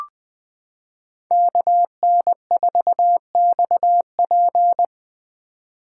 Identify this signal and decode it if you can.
Morse code, heard as a keyed tone, about 21 wpm, 700 Hz. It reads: KN4XP
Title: 2023 POTA Hunter Call Signs - 20wpm